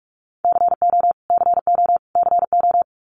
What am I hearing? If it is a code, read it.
Morse code, heard as a keyed tone, about 44 wpm, 700 Hz. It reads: CO CO CO
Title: US States Repeated 3x 45wpm